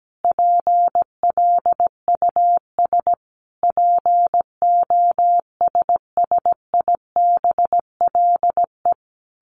Morse code, heard as a keyed tone, about 17 wpm, 700 Hz. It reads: PLUS POSSIBLE
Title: A Sound Of Thunder 17wpm